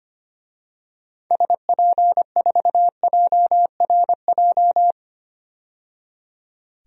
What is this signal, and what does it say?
Morse code, heard as a keyed tone, about 25 wpm, 700 Hz. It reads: SP4JRJ
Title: DXCC-100 Call Signs - 6 Characters - 25wpm